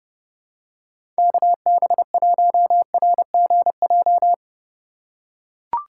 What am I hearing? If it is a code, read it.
Morse code, heard as a keyed tone, about 30 wpm, 700 Hz. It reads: KB1RGJ
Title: USA Call Signs - 6 Characters - 3x Character Spacing - 30wpm